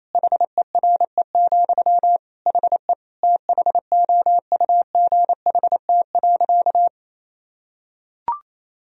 Morse code, heard as a keyed tone, about 28 wpm, 700 Hz. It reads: HERE, HE THOUGHT.
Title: Sentences from Top 1000 Words - Review Part 3 - 28wpm